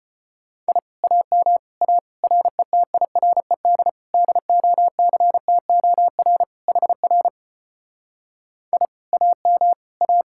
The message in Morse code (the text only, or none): I AM A RETIRED DOCTOR HR I AM A